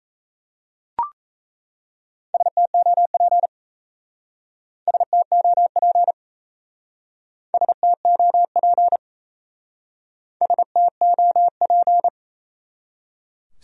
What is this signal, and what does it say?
Morse code, heard as a keyed tone, about 33 wpm, 700 Hz. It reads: STOP STOP STOP STOP
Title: Speed-Racing Top 500 Words 28wpm